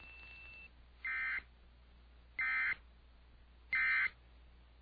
Emergency Alert System (SAME) tones, heard as harsh buzzing bursts. It is heard from 1.0 to 4.1 s.